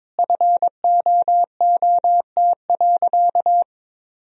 Morse code, heard as a keyed tone, about 22 wpm, 700 Hz. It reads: FOOT.
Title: Sentences from Top 1000 Words - Review Part 3 - 22wpm